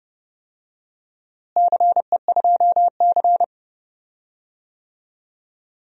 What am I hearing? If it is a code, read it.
Morse code, heard as a keyed tone, about 30 wpm, 700 Hz. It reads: CE2C
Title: DXCC-100 Call Signs - 4 Characters - Repeated 2x - 30wpm